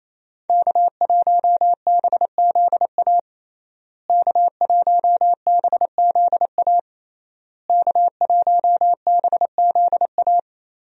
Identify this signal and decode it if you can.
Morse code, heard as a keyed tone, about 28 wpm, 700 Hz. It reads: K1BZA K1BZA K1BZA
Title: USA Call Signs - 5 Characters - Repeated 3x - 28wpm